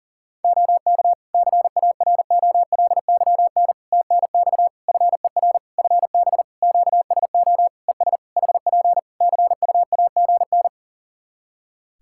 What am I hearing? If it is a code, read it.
Morse code, heard as a keyed tone, about 40 wpm, 700 Hz. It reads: OK CAROLYN TNX FER FB QSO ES HP CUAGN
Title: Practice QSO - Exchange 4 - Ending - 40wpm